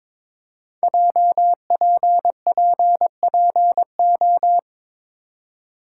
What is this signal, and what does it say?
Morse code, heard as a keyed tone, about 22 wpm, 700 Hz. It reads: JPPPO
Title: Code Groups - 5 Letters - 22wpm